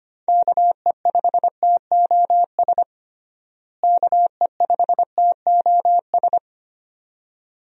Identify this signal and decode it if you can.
Morse code, heard as a keyed tone, about 25 wpm, 700 Hz. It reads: KE5TOS KE5TOS
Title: USA Call Signs - 6 Characters - Repeated 2x - 25wpm